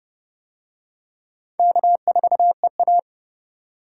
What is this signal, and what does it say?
Morse code, heard as a keyed tone, about 29 wpm, 700 Hz. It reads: K4EA